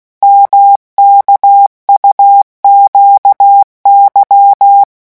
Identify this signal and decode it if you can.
Morse code, heard as a keyed tone, about 16 wpm, 800 Hz. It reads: MKUQY
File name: morse mp3 / revision2-vitesse 15 mots